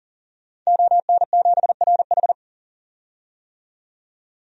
Morse code, heard as a keyed tone, about 40 wpm, 700 Hz. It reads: ON7RH
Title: DXCC-100 Call Signs - 5 Characters - 40wpm